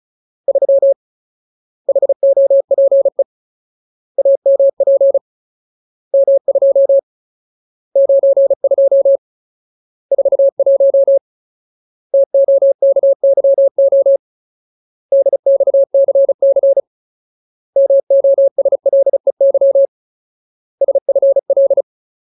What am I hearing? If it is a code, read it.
Morse code, heard as a keyed tone, about 34 wpm, 550 Hz. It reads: 3 HOPE AMP M2 92 41 TOKYO DXCC MOSLEY SFL
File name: Ultimate Word List / 35WPM Ultimate